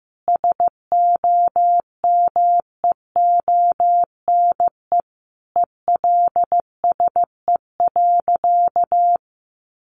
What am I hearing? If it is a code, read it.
Morse code, heard as a keyed tone, about 15 wpm, 700 Hz. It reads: SOMEONE ELSE.
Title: Sentences from Top 800 Words 15wpm